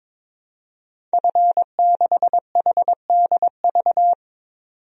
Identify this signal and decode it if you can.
Morse code, heard as a keyed tone, about 22 wpm, 700 Hz. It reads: F6HDV
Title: DXCC-100 Call Signs - 5 Characters - 22wpm